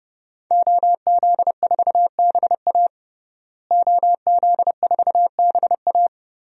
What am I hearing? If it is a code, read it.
Morse code, heard as a keyed tone, about 32 wpm, 700 Hz. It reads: OZ4BA OZ4BA
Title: DXCC-100 Call Signs - 5 Characters - Repeated 3x - 30wpm